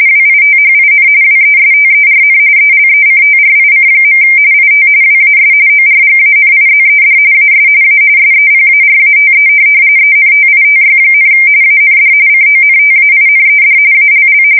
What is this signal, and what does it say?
RTTY, about 45.45 baud. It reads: WITHIN WESTERN ELECTRIC. IT WAS ORGANIZED AS A SEPARATE ENTITY, AND CONTAINED ALL
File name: fsk170 / teletype-corp